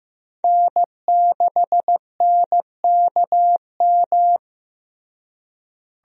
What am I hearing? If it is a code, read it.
Morse code, heard as a keyed tone, about 16 wpm, 700 Hz. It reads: N6NKM
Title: USA Call Signs - 5 Characters - Repeated 2x - 15wpm